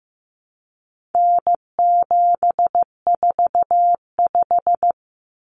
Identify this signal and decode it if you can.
Morse code, heard as a keyed tone, about 15 wpm, 700 Hz. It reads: N745
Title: Common Cut Numbers - 4-Digit 15wpm